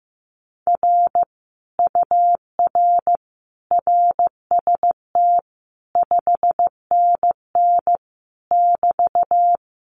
Morse code, heard as a keyed tone, about 15 wpm, 700 Hz. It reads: R UR RST 5NN <BT>
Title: Common QSO Phrases Encore 15wpm